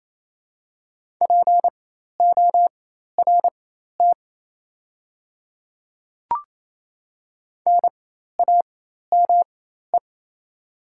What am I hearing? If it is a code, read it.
Morse code, heard as a keyed tone, about 28 wpm, spaced as for 13 wpm, 700 Hz. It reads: PORT NAME
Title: 4x Character Spacing - Top 200 Words - 28wpm